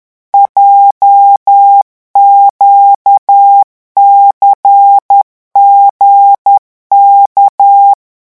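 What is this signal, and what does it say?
Morse code, heard as a keyed tone, about 10 wpm, 800 Hz. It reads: JQCGK
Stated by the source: Morse revision exercise for French radio amateurs at 10 words per minute.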